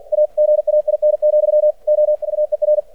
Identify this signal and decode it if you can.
Morse code, heard as a keyed tone, about 50 wpm, 600 Hz. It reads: AGAIN, O AER
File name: archived / CRSnet